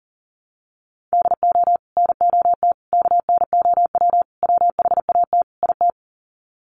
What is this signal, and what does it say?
Morse code, heard as a keyed tone, about 40 wpm, 700 Hz. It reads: DO NOT KNOW WHAT IT
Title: Sets of 5 Words 40wpm